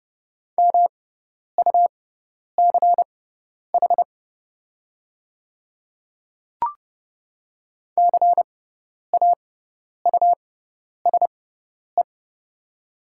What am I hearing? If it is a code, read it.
Morse code, heard as a keyed tone, about 30 wpm, spaced as for 10 wpm, 700 Hz. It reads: MUCH CAUSE
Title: 6x Character Spacing - Top 200 Words - 30wpm